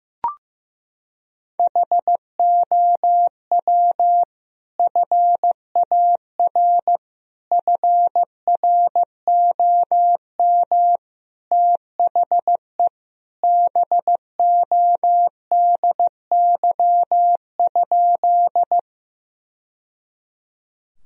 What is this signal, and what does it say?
Morse code, heard as a keyed tone, about 15 wpm, 700 Hz. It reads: HOW FAR FROM THE BODY?
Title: Sentences from Top 1000 Words - Review Part 2 - 15wpm